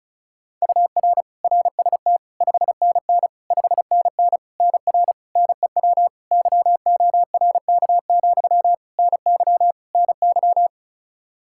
Morse code, heard as a keyed tone, about 36 wpm, 700 Hz. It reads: UR RST 5NN 5NN NR NEW YORK, NY NY
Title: K1USN SST QSO Practice 35wpm